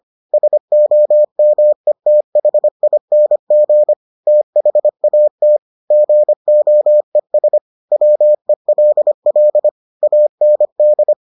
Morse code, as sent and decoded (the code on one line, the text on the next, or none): ... --- -- . - .... .. -. --. / - .... .- - / --. --- . ... / .-- . .-.. .-.. / .- -. -..
SOMETHING THAT GOES WELL AND